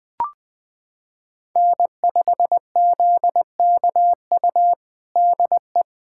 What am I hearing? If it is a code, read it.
Morse code, heard as a keyed tone, about 21 wpm, 700 Hz. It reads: N5ZKU DE
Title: USA Callsign DE Callsign - 5 Characters - 20wpm